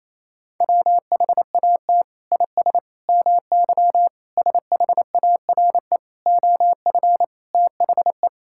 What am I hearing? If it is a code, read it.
Morse code, heard as a keyed tone, about 27 wpm, 700 Hz. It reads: WHAT IS MY SHARE OF THE
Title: Sentences from Top 1000 Words - Review Part 6 - 28wpm